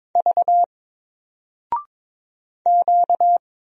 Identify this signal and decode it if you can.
Morse code, heard as a keyed tone, about 22 wpm, 700 Hz. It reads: V Q
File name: Single-Letter-Number-22wpm